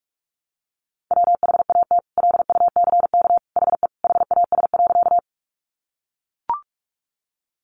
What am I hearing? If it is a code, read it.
Morse code, heard as a keyed tone, about 44 wpm, 700 Hz. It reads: WHAT LUCK HE HAS.
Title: Sentences from Top 1500 Words 45wpm